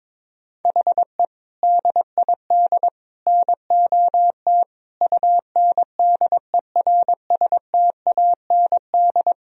Morse code, heard as a keyed tone, about 22 wpm, 700 Hz. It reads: HE DID NOT UNDERSTAND
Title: Sentences from Top 1000 Words - Review Part 3 - 22wpm